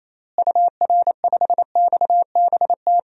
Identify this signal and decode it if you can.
Morse code, heard as a keyed tone, about 28 wpm, 700 Hz. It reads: UR5XBT
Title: DXCC-100 Call Signs - 6 Characters - Repeated 3x - 28wpm